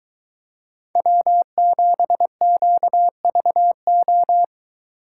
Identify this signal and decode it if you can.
Morse code, heard as a keyed tone, about 23 wpm, 700 Hz. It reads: W7QVO